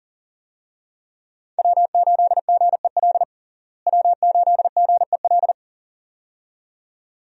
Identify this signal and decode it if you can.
Morse code, heard as a keyed tone, about 40 wpm, 700 Hz. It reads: W8GEL W8GEL
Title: USA Call Signs - 5 Characters - Repeated 2x - 40wpm